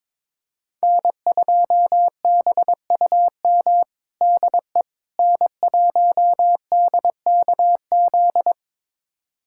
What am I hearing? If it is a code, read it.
Morse code, heard as a keyed tone, about 22 wpm, 700 Hz. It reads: N2BUM DE N1DKZ
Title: USA Callsign DE Callsign - 5 Characters - 22wpm